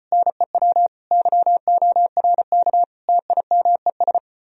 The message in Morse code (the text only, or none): NEW YORK TIMES